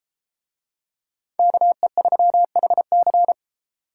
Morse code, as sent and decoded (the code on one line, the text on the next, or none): -.- . ...-- .... -.-.
KE3HC